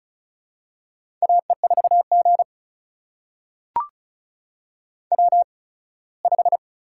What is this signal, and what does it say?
Morse code, heard as a keyed tone, about 36 wpm, spaced as for 21 wpm, 700 Hz. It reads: AE4G W5
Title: USA Call Signs - 4 Characters - 8x Character Spacing - 35wpm